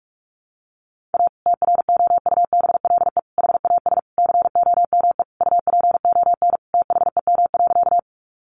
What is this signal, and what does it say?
Morse code, heard as a keyed tone, about 44 wpm, 700 Hz. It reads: A TROUBLE HAS COME UPON THEM.
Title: Sentences from Top 1200 Words 45wpm